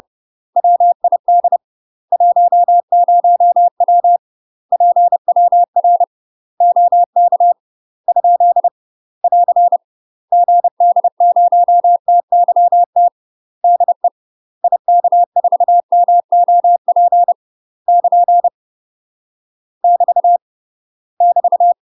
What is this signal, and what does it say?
Morse code, heard as a keyed tone, about 30 wpm, 700 Hz. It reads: WID 10W PWR OK ? <AR> GD0TYT DE IK4MOP <KN> <BT> <BT>
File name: UK MASTER QSOs TRAINING / 30WPM / UK MASTER QSOs 0050